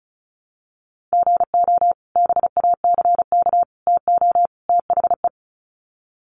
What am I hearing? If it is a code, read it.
Morse code, heard as a keyed tone, about 34 wpm, 700 Hz. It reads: GO BACK TO THE
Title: Sets of 4 Words 35wpm